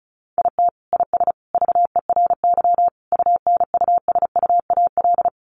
Morse code, heard as a keyed tone, about 34 wpm, 700 Hz. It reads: IT IS VERY UNUSUAL
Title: Sentences from Top 1400 Words 35wpm